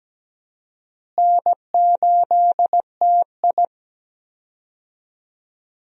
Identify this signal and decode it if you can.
Morse code, heard as a keyed tone, about 17 wpm, 700 Hz. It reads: N8TI
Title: USA Call Signs - 4 Characters - Repeated 2x - 17wpm